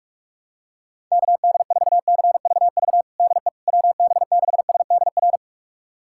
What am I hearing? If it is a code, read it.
Morse code, heard as a keyed tone, about 41 wpm, 700 Hz. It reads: KD4CVV DE WB6SDR
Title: USA Callsign DE Callsign - 6 Characters - 45wpm